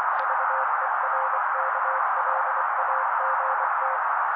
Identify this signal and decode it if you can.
Morse code, heard as a keyed tone, about 24 wpm, 530 Hz. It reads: VERKLAGT
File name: daily / 23wpm noise